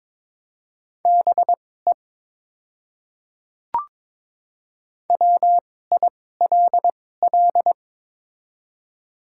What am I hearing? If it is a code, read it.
Morse code, heard as a keyed tone, about 22 wpm, spaced as for 16 wpm, 700 Hz. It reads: BE WILL
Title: 2x Character Spacing - Top 200 Words - 22wpm